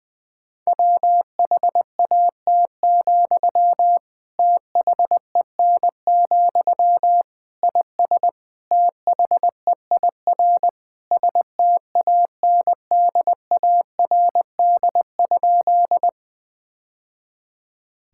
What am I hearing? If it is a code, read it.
Morse code, heard as a keyed tone, about 20 wpm, 700 Hz. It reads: WHAT, THEN, IS THEIR STANDARD?
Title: Sentences from Top 1000 Words - Review Part 6 - 20wpm